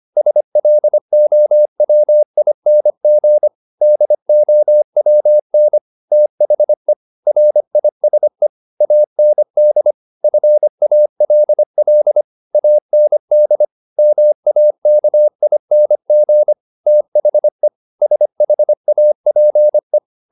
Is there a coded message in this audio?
Yes: Morse code from 0.2 to 20.0 s, heard as a keyed tone, about 25 wpm, 600 Hz. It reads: SLOWING DOWN THE RISE AND FALL AND MAKING THE SHAPE